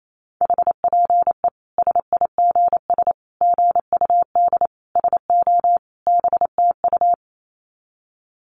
Morse code, heard as a keyed tone, about 28 wpm, 700 Hz. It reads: HPE SIGS GUD SO BTU
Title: Common QSO Phrases 28wpm